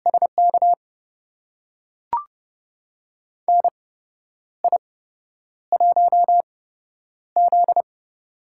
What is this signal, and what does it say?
Morse code, heard as a keyed tone, about 30 wpm, spaced as for 10 wpm, 700 Hz. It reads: SK NI1Z
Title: USA Call Signs - 4 Characters - 8x Character Spacing - 30wpm